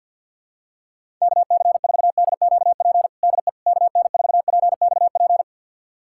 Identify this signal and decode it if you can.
Morse code, heard as a keyed tone, about 51 wpm, 700 Hz. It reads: KC4DQP DE KN4PXP
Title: USA Callsign DE Callsign 50wpm